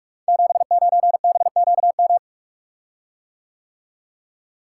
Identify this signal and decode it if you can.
Morse code, heard as a keyed tone, about 44 wpm, 700 Hz. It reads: 79BQM
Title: Code Groups - 5 Letter:Numbers - 45wpm